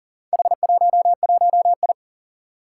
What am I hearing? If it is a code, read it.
Morse code, heard as a keyed tone, about 42 wpm, 700 Hz. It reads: H11I
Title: Four Challenging Letter-Number Combinations 40wpm